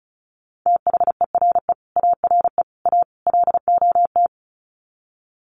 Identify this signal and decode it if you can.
Morse code, heard as a keyed tone, about 34 wpm, 700 Hz. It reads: THERE ARE A LOT
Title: Sets of 4 Words 35wpm